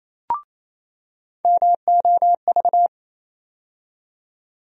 Morse code, heard as a keyed tone, about 28 wpm, 700 Hz. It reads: MOV
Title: USA Call Sign Postfixes - 3 Letters 28wpm